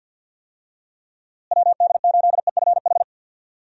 Morse code, heard as a keyed tone, about 47 wpm, 700 Hz. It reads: WD8EFH
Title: USA Call Signs - 6 Characters - Repeated 3x - 50wpm